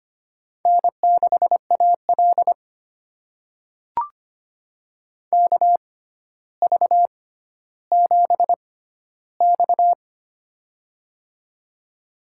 Morse code, heard as a keyed tone, about 25 wpm, spaced as for 13 wpm, 700 Hz. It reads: N6AL KV7X
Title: USA Call Signs - 4 Characters - 6x Character Spacing - 25wpm